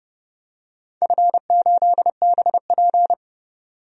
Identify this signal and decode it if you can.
Morse code, heard as a keyed tone, about 30 wpm, 700 Hz. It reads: F8BP
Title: DXCC-100 Call Signs - 4 Characters - 30wpm